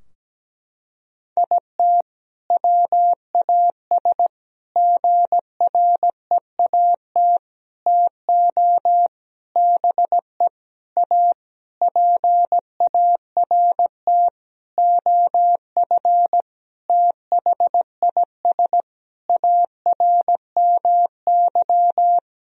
Morse code, heard as a keyed tone, about 17 wpm, 700 Hz. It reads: IT WAS GREAT TO BE A PART OF THIS ARMY
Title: Sentences from Top 1000 Words - Review Part 2 - 17wpm